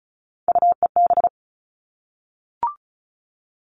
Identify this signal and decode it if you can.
Morse code, heard as a keyed tone, about 34 wpm, 700 Hz. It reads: UEB
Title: All Cut Numbers - 3-Digit 35wpm